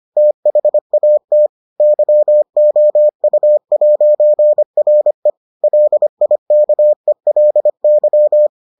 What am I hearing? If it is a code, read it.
Morse code, heard as a keyed tone, about 25 wpm, 600 Hz. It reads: THAT YOU'RE LIKELY